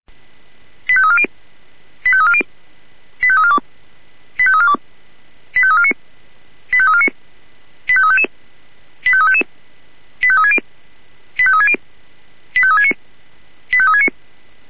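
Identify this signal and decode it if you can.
Five-tone selective call (ZVEI), heard as quick runs of five tones, tones of about 70 ms. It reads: 85340 85340 85342 85342 85349 85349 85355 85355 85360 85360 85369 85369
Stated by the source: Five-tone ZVEI selective call 85369, received by SDR.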